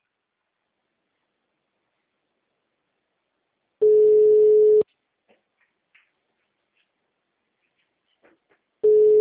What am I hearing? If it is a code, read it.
Ringback tone (Europe, 425 Hz).